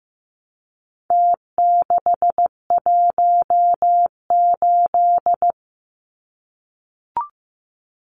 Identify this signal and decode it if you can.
Morse code, heard as a keyed tone, about 15 wpm, 700 Hz. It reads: T618
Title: Numbers - Contest Style - 4-Digit 15wpm